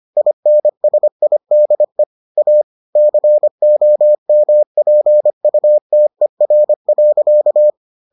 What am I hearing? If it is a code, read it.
Morse code, heard as a keyed tone, about 25 wpm, 600 Hz. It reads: INSIDE A COMPUTER.